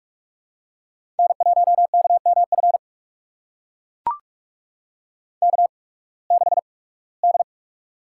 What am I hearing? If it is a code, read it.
Morse code, heard as a keyed tone, about 42 wpm, spaced as for 23 wpm, 700 Hz. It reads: N1KMF K6D
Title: USA Call Signs - 8x Character Spacing - 45wpm